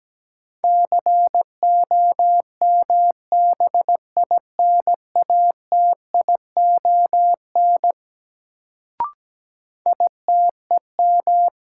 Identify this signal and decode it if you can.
Morse code, heard as a keyed tone, about 17 wpm, 700 Hz. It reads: COMBINATION ITEM
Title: Sentences from Top 1000 Words - Review Part 6 - New Word Drill - 17wpm